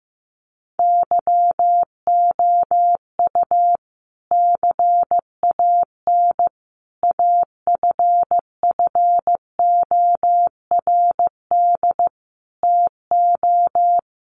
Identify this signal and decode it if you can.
Morse code, heard as a keyed tone, about 15 wpm, 700 Hz. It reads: YOU CAN AFFORD TO
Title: Sentences from Top 1100 Words 15wpm